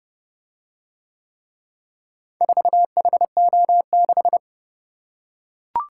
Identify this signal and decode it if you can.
Morse code, heard as a keyed tone, about 29 wpm, 700 Hz. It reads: 4HO6